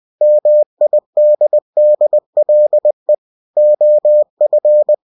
Morse code, heard as a keyed tone, about 20 wpm, 600 Hz. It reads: MIDDLE OF
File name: cw-books / winnie-the-pooh / winnie-20wpm-page-0093